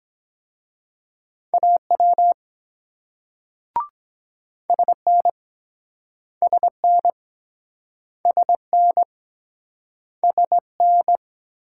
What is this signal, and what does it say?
Morse code, heard as a keyed tone, about 22 wpm, 700 Hz. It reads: AW SN SN SN SN